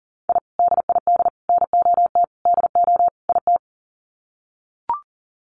Morse code, heard as a keyed tone, about 38 wpm, 700 Hz. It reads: I DID NOT DO IT
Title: Sets of 5 Words Encore2 40wpm